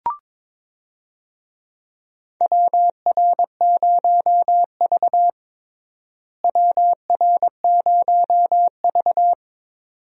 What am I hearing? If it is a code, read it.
Morse code, heard as a keyed tone, about 22 wpm, 700 Hz. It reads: WR0V WR0V
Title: USA Call Signs - 4 Characters - Repeated 2x - 22wpm